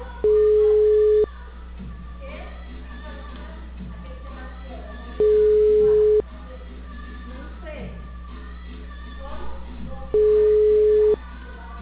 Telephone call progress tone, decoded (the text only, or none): ringback tone